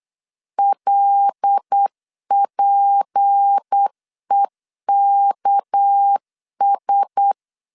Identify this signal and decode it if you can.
Morse code, heard as a keyed tone, about 8 wpm, 800 Hz. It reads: LPEKS
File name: LEZIONI / 6 PKX0 / 6 PK 8wpm 3003